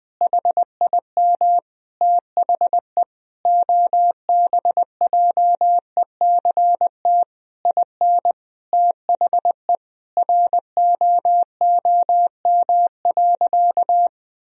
Morse code, heard as a keyed tone, about 20 wpm, 700 Hz. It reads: HIM THE OBJECT IN THE ROOM.
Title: Sentences from Top 1000 Words - Review Part 3 - 20wpm